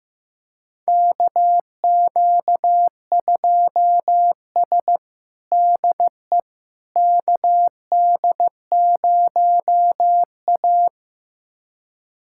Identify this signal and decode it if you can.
Morse code, heard as a keyed tone, about 15 wpm, 700 Hz. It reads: KQ2S DE KD0A
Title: USA Callsign DE Callsign - 4 Characters - 15wpm